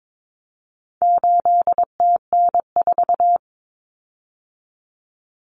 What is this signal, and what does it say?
Morse code, heard as a keyed tone, about 22 wpm, 700 Hz. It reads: 8TN4
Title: Common Cut Numbers - 4-Digit 22wpm